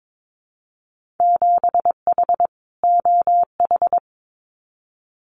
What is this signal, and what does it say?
Morse code, heard as a keyed tone, about 22 wpm, 700 Hz. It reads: 7H OH
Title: Winter Field Day Exchanges Encore 22wpm